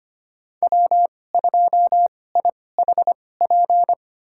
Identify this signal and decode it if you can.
Morse code, heard as a keyed tone, about 25 wpm, spaced as for 18 wpm, 700 Hz. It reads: W2IHP
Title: USA Call Signs - 5 Characters - 2x Character Spacing - 25wpm